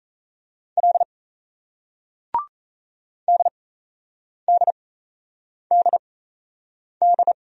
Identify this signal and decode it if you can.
Morse code, heard as a keyed tone, about 36 wpm, spaced as for 8 wpm, 700 Hz. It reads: L DDDD